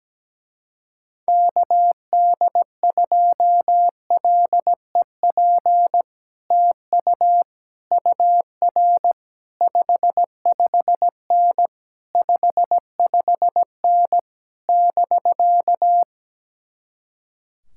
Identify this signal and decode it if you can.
Morse code, heard as a keyed tone, about 17 wpm, 700 Hz. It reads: KD2LEP TU UR 55N 55N <BK>
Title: Speed-Racing Common QSO Phrases Encore - 17wpm